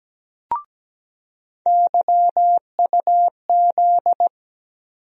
Morse code, heard as a keyed tone, about 17 wpm, 700 Hz. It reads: YUZ
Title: USA Call Sign Postfixes - 3 Letters 17wpm